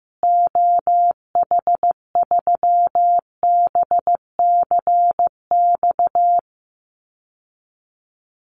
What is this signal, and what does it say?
Morse code, heard as a keyed tone, about 15 wpm, 700 Hz. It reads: OH3BCX
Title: Call Signs Repeated 3x - 15wpm